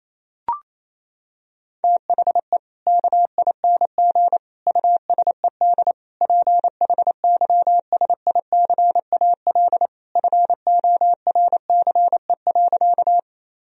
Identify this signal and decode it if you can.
Morse code, heard as a keyed tone, about 28 wpm, 700 Hz. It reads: THE KING USED PHYSICAL FORCE.
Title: Sentences from Top 1000 Words - Review Part 4 - 28wpm